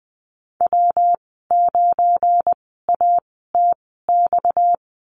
Morse code, heard as a keyed tone, about 20 wpm, spaced as for 14 wpm, 700 Hz. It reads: W9ATX
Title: 238 Lesson 38g - Letter q - Call Signs 20wpm